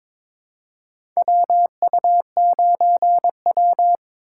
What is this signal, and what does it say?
Morse code, heard as a keyed tone, about 22 wpm, 700 Hz. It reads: WU9W
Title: USA Call Signs - 4 Characters - 4x Character Spacing - 22wpm